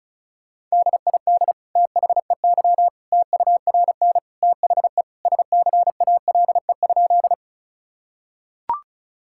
Morse code, heard as a keyed tone, about 34 wpm, 700 Hz. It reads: DID THEY TURN THE SCALE?
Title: Sentences from Top 1000 Words - Review Part 5 - 35wpm